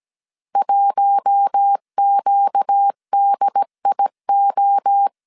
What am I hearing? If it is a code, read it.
Morse code, heard as a keyed tone, about 17 wpm, 800 Hz. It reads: 1QDIO
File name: CW / DET / 1 9 1500C S A 16wpm 2828